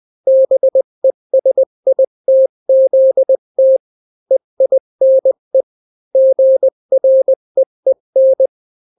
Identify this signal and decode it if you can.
Morse code, heard as a keyed tone, about 20 wpm, 530 Hz. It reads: BESITZT EINE GREEN
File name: daily / 18wpm